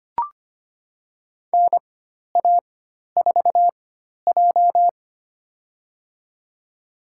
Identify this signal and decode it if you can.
Morse code, heard as a keyed tone, about 25 wpm, spaced as for 12 wpm, 700 Hz. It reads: NA4J
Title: USA Call Signs - 4 Characters - 4x Character Spacing - 25wpm